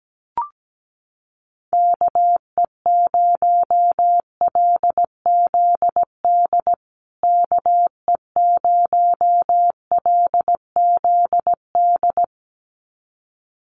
Morse code, heard as a keyed tone, about 17 wpm, 700 Hz. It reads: KE0LZD KE0LZD
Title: USA Call Signs Repeated 2x 17wpm